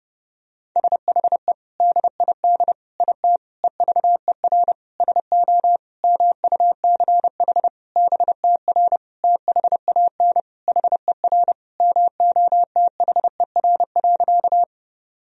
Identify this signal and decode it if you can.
Morse code, heard as a keyed tone, about 32 wpm, 700 Hz. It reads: SHE DID IT EVER SO MUCH BTR THAN HER MOTHER.
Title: Sentences from Top 400 Words and Abbreviations 30wpm